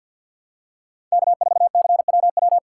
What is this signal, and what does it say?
Morse code, heard as a keyed tone, about 53 wpm, 700 Hz. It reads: K4CWW
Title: USA Call Signs - 5 Characters - 2x Character Spacing - 50wpm